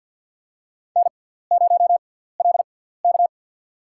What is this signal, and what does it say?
Morse code, heard as a keyed tone, about 46 wpm, spaced as for 17 wpm, 700 Hz. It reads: N0LK
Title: USA Call Signs - 4 Characters - 6x Character Spacing - 50wpm